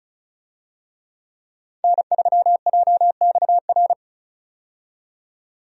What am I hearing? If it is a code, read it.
Morse code, heard as a keyed tone, about 36 wpm, 700 Hz. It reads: N3JXR
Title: USA Call Signs - 5 Characters - 35wpm